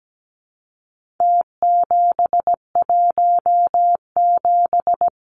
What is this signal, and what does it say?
Morse code, heard as a keyed tone, about 17 wpm, 700 Hz. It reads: T717
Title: Numbers - Contest Style - 4-Digit 17wpm